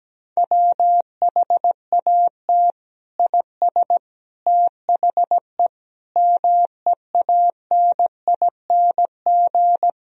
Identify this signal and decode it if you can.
Morse code, heard as a keyed tone, about 17 wpm, 700 Hz. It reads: WHAT IS THE MEANING
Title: Sentences from Top 1000 Words - Review Part 4 - 17wpm